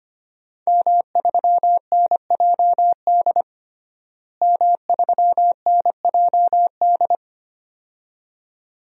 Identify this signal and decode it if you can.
Morse code, heard as a keyed tone, about 25 wpm, 700 Hz. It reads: M3NJD M3NJD
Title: DXCC-100 Call Signs - 5 Characters - Repeated 2x - 25wpm